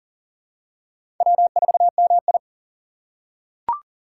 Morse code, heard as a keyed tone, about 40 wpm, 700 Hz. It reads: W4MI
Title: USA Call Signs - 4 Characters - 3x Character Spacing - 40wpm